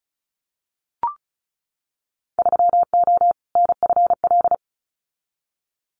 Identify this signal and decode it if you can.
Morse code, heard as a keyed tone, about 34 wpm, 700 Hz. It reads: 3O NFL
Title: Winter Field Day Exchanges 35wpm